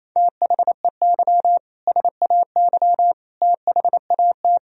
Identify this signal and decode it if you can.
Morse code, heard as a keyed tone, about 28 wpm, 700 Hz. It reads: THEY SAY THAT
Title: Sentences from Top 1000 Words - Review Part 3 - 28wpm